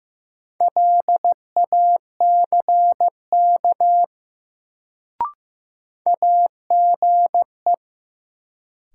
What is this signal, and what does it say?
Morse code, heard as a keyed tone, about 15 wpm, 700 Hz. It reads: LACK AGE
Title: Sets of 3 Words - New Word Drill - 15wpm